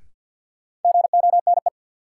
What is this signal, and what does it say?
Morse code, heard as a keyed tone, about 52 wpm, 700 Hz. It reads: GONE
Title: Sentences from Top 1200 Words - New Word Drill - 50wpm